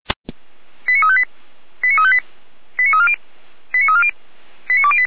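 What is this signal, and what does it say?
Five-tone selective call (ZVEI), heard as quick runs of five tones, tones of about 80 ms. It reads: 89358 89358 89340 89340 89208
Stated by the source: ZVEI five-tone selective call received by SDR, recorded by Feuerwehr Orlamünde.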